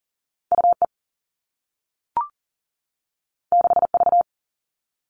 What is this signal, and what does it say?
Morse code, heard as a keyed tone, about 42 wpm, 700 Hz. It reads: UE 6V
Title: All Cut Numbers - 2-Digit 40wpm